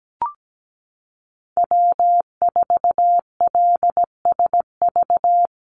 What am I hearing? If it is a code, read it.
Morse code, heard as a keyed tone, about 17 wpm, 700 Hz. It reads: W4LSV
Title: Call Signs Repeated 2x - 17wpm